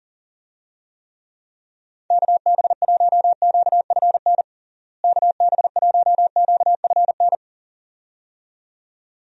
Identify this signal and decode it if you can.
Morse code, heard as a keyed tone, about 40 wpm, 700 Hz. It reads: KB1QFN KB1QFN
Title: USA Call Signs - 6 Characters - Repeated 2x - 40wpm